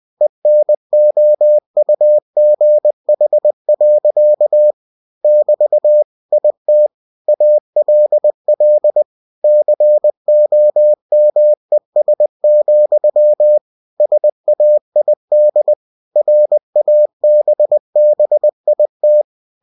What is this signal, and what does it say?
Morse code, heard as a keyed tone, about 20 wpm, 600 Hz. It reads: ENOUGH. <BT> IT ALL COMES, SAID RABBIT
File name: cw-books / winnie-the-pooh / winnie-20wpm-page-0017